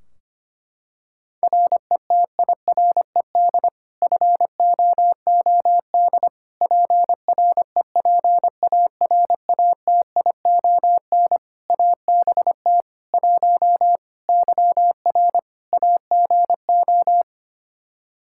Morse code, heard as a keyed tone, about 25 wpm, 700 Hz. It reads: RETIRED FOOD PREPARATION ABT 1 YR AGO
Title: Practice QSO - Exchange 3 - More Info - 25wpm